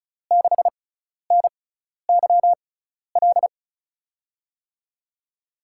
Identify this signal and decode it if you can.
Morse code, heard as a keyed tone, about 34 wpm, spaced as for 12 wpm, 700 Hz. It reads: 6NYL
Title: USA Call Signs - 6 Characters - 6x Character Spacing - 35wpm